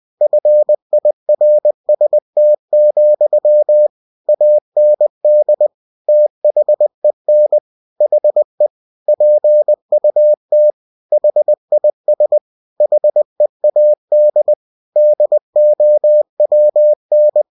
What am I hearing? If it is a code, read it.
Morse code, heard as a keyed tone, about 20 wpm, 600 Hz. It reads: FIRST, AND THEN HE PUT HIS HEAD DOWN
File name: cw-books / winnie-the-pooh / winnie-20wpm-page-0027